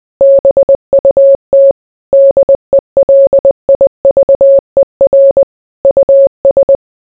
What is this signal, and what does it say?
Morse code, heard as a keyed tone, about 20 wpm, 550 Hz. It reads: BUT DELIVER US